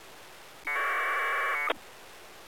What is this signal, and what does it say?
Packet radio (AX.25, as in APRS).